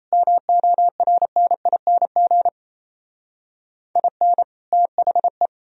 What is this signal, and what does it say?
Morse code, heard as a keyed tone, about 33 wpm, 700 Hz. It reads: MORNING IN THE
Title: Speed-Racing - Sets of 3 Words 28wpm